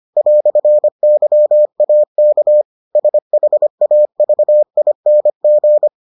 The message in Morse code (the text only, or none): "YAK SHAVING